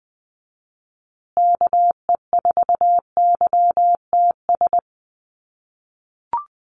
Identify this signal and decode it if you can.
Morse code, heard as a keyed tone, about 20 wpm, 700 Hz. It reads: KE4YTS
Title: USA Call Signs Repeated 3x 20wpm